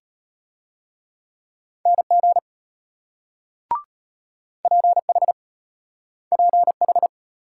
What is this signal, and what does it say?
Morse code, heard as a keyed tone, about 33 wpm, 700 Hz. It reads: NG PH PH